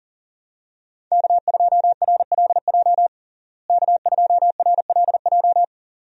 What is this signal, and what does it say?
Morse code, heard as a keyed tone, about 40 wpm, 700 Hz. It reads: K2RLJ K2RLJ
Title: USA Call Signs - 5 Characters - Repeated 2x - 40wpm